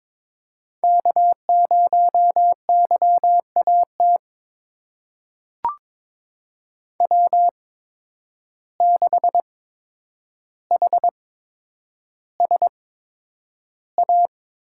Morse code, heard as a keyed tone, about 22 wpm, spaced as for 9 wpm, 700 Hz. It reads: K0YAT W6HSA
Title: USA Call Signs - 5 Characters - 8x Character Spacing - 22wpm